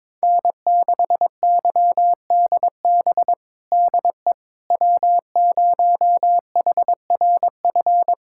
Morse code, heard as a keyed tone, about 22 wpm, 700 Hz. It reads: N6YDB DE W0HRF
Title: USA Callsign DE Callsign - 5 Characters - 22wpm